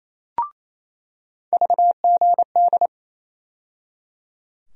Morse code, heard as a keyed tone, about 28 wpm, 700 Hz. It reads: VGD
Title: USA Call Sign Postfixes - 3 Letters 28wpm